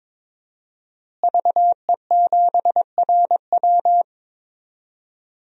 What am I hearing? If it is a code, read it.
Morse code, heard as a keyed tone, about 21 wpm, 700 Hz. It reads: VE7RW
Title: DXCC-100 Call Signs - 5 Characters - Repeated 3x - 22wpm